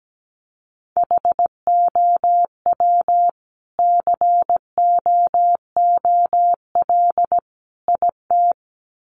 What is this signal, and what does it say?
Morse code, heard as a keyed tone, about 17 wpm, 700 Hz. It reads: HOW COOL IT
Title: Sentences from Top 1200 Words 17wpm